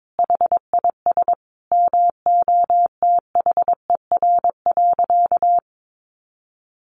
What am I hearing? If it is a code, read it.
Morse code, heard as a keyed tone, about 22 wpm, 700 Hz. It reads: HIS MOTHER.
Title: Sentences from Top 1300 Words 22wpm